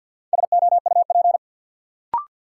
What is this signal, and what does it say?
Morse code, heard as a keyed tone, about 47 wpm, 700 Hz. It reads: SOUP
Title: Sentences from Top 1000 Words - Review Part 6 - New Word Drill - 50wpm